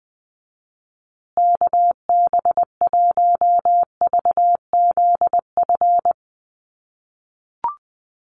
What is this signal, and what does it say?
Morse code, heard as a keyed tone, about 20 wpm, 700 Hz. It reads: KB1VZF
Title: USA Call Signs Repeated 2x 20wpm